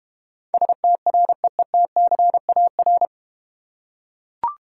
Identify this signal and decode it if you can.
Morse code, heard as a keyed tone, about 32 wpm, 700 Hz. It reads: STREETCAR